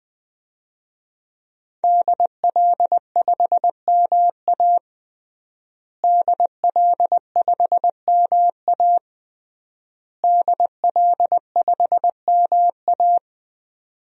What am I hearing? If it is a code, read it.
Morse code, heard as a keyed tone, about 20 wpm, 700 Hz. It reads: DL5MA DL5MA DL5MA
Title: DXCC-100 Call Signs - 5 Characters - Repeated 3x - 20wpm